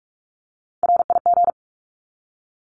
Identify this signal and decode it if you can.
Morse code, heard as a keyed tone, about 44 wpm, 700 Hz. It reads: RIG
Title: Abbreviations 45wpm